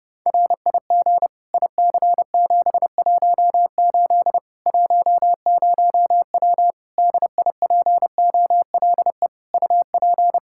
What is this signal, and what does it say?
Morse code, heard as a keyed tone, about 32 wpm, 700 Hz. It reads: RIG IC718 10W DIPOLE UP
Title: Practice QSO - Exchange 2 - Info - 30wpm